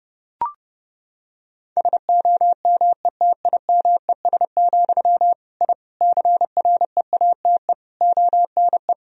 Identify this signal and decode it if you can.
Morse code, heard as a keyed tone, about 30 wpm, 700 Hz. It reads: SOMETIMES, I CREATE ONE
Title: Sentences from Top 1000 Words - Review Part 3 - 30wpm